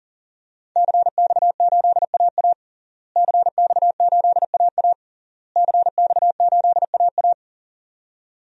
Morse code, heard as a keyed tone, about 40 wpm, 700 Hz. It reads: CX8AA CX8AA CX8AA
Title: DXCC-100 Call Signs - 5 Characters - Repeated 3x - 40wpm